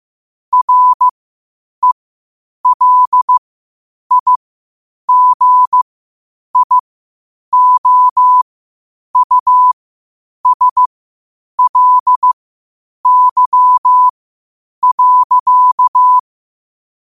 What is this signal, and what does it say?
Morse code, heard as a keyed tone, about 15 wpm, spaced as for 8 wpm, 1000 Hz. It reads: RELIGIOUSLY.
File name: qotdfiles / 8wpm